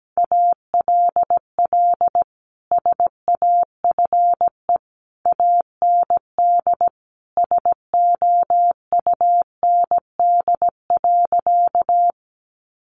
Morse code, heard as a keyed tone, about 17 wpm, 700 Hz. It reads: ALL SAFE AND SOUND.
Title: Sentences from Top 900 Words 17wpm